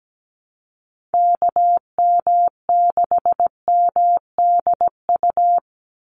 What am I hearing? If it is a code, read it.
Morse code, heard as a keyed tone, about 17 wpm, 700 Hz. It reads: KM6MDU
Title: USA Call Signs 17wpm